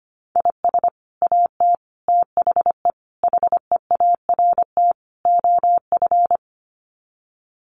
Morse code, heard as a keyed tone, about 25 wpm, 700 Hz. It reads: IS AT THE HEART OF
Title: Sets of 5 Words Encore2 25wpm